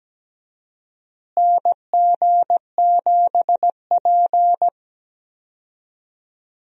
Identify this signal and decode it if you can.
Morse code, heard as a keyed tone, about 17 wpm, 700 Hz. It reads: NG7P
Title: USA Call Signs - 4 Characters - Repeated 2x - 17wpm